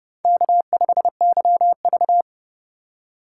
Morse code, heard as a keyed tone, about 29 wpm, 700 Hz. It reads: K5YV